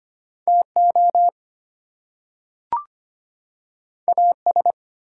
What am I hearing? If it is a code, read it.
Morse code, heard as a keyed tone, about 25 wpm, 700 Hz. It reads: TO AS
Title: Two Letter Words - 25wpm